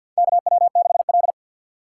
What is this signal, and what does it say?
Morse code, heard as a keyed tone, about 53 wpm, 700 Hz. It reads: KW6L
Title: USA Call Signs - 3x Character Spacing - 50wpm